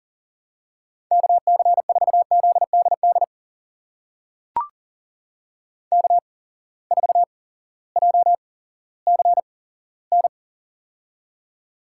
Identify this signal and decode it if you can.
Morse code, heard as a keyed tone, about 40 wpm, spaced as for 18 wpm, 700 Hz. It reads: KC4ZDD K4JCN
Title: USA Call Signs - 8x Character Spacing - 40wpm